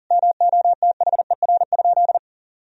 Morse code, heard as a keyed tone, about 40 wpm, 700 Hz. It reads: MOTHER?
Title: Sentences from Top 1000 Words - Review Part 2 - 40wpm